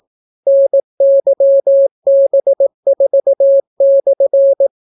Morse code, heard as a keyed tone, about 18 wpm, 550 Hz. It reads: NYB4/